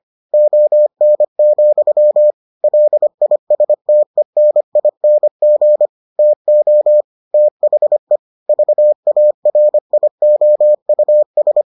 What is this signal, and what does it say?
Morse code, heard as a keyed tone, about 25 wpm, 600 Hz. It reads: ON, LISTENING TO THE VARIOUS